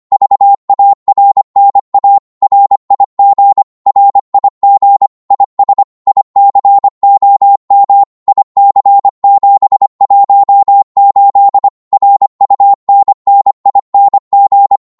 Morse code, heard as a keyed tone, about 25 wpm, 800 Hz. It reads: VARNA RIG RIG IS ICOM IC718 RUNNING
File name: Morse / qso-25wpm